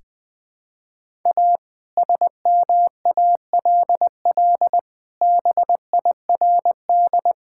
Morse code, heard as a keyed tone, about 20 wpm, 700 Hz. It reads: A SMALL BIRD
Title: Sentences with 2-Word Phrases - 20wpm